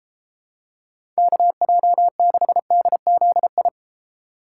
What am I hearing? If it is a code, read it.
Morse code, heard as a keyed tone, about 32 wpm, 700 Hz. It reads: KJ6DZI